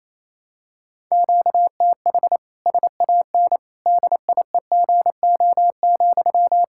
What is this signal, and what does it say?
Morse code, heard as a keyed tone, about 28 wpm, 700 Hz. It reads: QTH SAN DIEGO,
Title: Practice QSO - Exchange 1 - Beginning - 28wpm